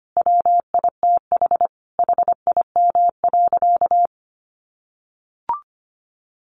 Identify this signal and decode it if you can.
Morse code, heard as a keyed tone, about 25 wpm, 700 Hz. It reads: WITH HIM.
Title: Sentences from Top 1100 Words 25wpm